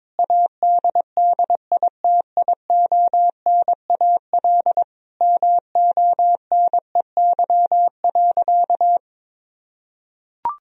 Morse code, heard as a keyed tone, about 22 wpm, 700 Hz. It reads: ADDITIONAL MONEY.
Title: Sentences from Top 1000 Words - Review Part 4 - 22wpm